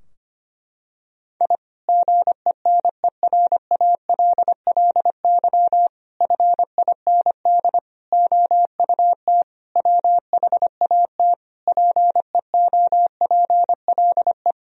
Morse code, heard as a keyed tone, about 25 wpm, 700 Hz. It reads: I GENERALLY FIND OUT WHAT PEOPLE
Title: Sentences from Top 1000 Words - Review Part 4 - 25wpm